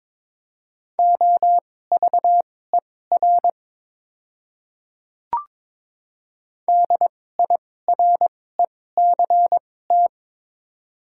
Morse code, heard as a keyed tone, about 22 wpm, spaced as for 16 wpm, 700 Hz. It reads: OVER DIRECT
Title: 2x Character Spacing - Top 500 Words - 22wpm